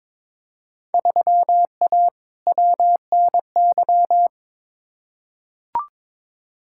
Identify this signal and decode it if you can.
Morse code, heard as a keyed tone, about 22 wpm, 700 Hz. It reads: 3A WNY
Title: Field Day Exchanges - 22wpm